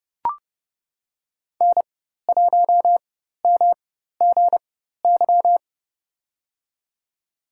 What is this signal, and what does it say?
Morse code, heard as a keyed tone, about 30 wpm, spaced as for 14 wpm, 700 Hz. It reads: N1MGY
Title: USA Call Signs - 5 Characters - 4x Character Spacing - 30wpm